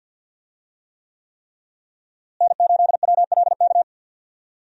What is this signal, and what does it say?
Morse code, heard as a keyed tone, about 52 wpm, 700 Hz. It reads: N8WLK
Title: USA Call Signs - 5 Characters - Repeated 3x - 50wpm